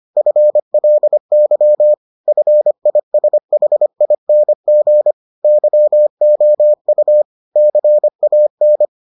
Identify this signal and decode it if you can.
Morse code, heard as a keyed tone, about 25 wpm, 600 Hz. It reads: FLY FISHING YOU CAN